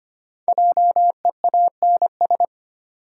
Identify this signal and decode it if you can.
Morse code, heard as a keyed tone, about 25 wpm, 700 Hz. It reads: JEANS
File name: 5 Letter Words - First 3 Letters Given - 25wpm